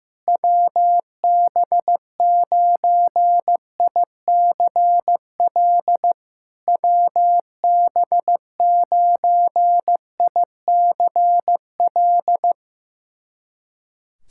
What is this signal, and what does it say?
Morse code, heard as a keyed tone, about 15 wpm, 700 Hz. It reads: WB9ICL WB9ICL
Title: SOTA QSO Practice 15wpm